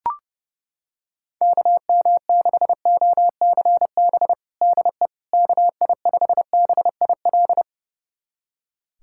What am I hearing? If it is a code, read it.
Morse code, heard as a keyed tone, about 30 wpm, 700 Hz. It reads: KM6OCB DE KI5BIL
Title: USA Callsign DE Callsign - 6 Characters - 30wpm